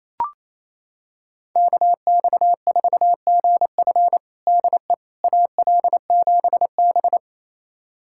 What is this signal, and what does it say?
Morse code, heard as a keyed tone, about 28 wpm, 700 Hz. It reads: KX4GF DE AL7B
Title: USA Callsign DE Callsign 28wpm